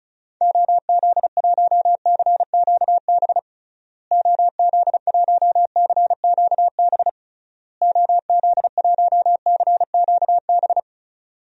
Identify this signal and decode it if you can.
Morse code, heard as a keyed tone, about 34 wpm, 700 Hz. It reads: OZ1CQB OZ1CQB OZ1CQB
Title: DXCC-100 Call Signs - 6 Characters - Repeated 3x - 35wpm